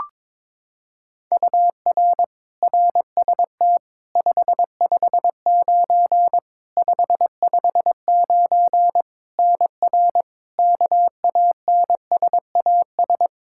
Morse code, heard as a keyed tone, about 21 wpm, 700 Hz. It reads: UR RST 559 559 NR KANSAS
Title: K1USN SST QSO Practice 22wpm